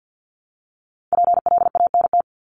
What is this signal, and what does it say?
Morse code, heard as a keyed tone, about 50 wpm, 700 Hz. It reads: PLANT
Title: Top 300 Words 50wpm